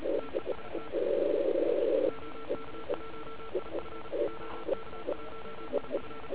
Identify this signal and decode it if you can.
Morse code, heard as a keyed tone, about 34 wpm, 1340 Hz, with noise. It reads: TEST OK1IF OK1IF